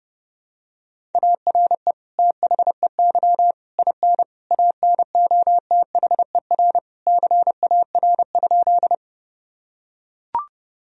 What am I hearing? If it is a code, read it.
Morse code, heard as a keyed tone, about 30 wpm, 700 Hz. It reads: ARE THEY IN ANOTHER CAR?
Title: Sentences from Top 1000 Words - Review Part 1 - 30wpm